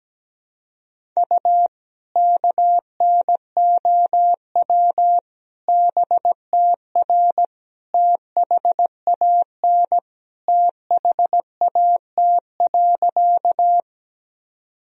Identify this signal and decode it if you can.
Morse code, heard as a keyed tone, about 17 wpm, 700 Hz. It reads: U KNOW BTR THAN THAT.
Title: Sentences from Top 400 Words and Abbreviations 17wpm